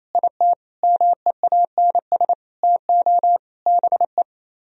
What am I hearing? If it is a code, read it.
Morse code, heard as a keyed tone, about 36 wpm, 700 Hz. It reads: IT MEANS TO BE